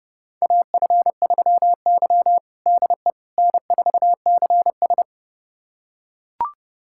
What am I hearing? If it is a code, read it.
Morse code, heard as a keyed tone, about 30 wpm, 700 Hz. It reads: AF3Y DE N4CS
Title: USA Callsign DE Callsign - 4 Characters - 30wpm